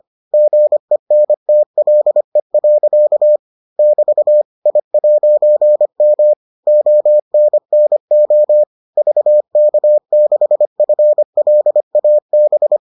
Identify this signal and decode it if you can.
Morse code, heard as a keyed tone, about 25 wpm, 600 Hz. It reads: GENTLE. <BT> I'M ONNO VK6FLAB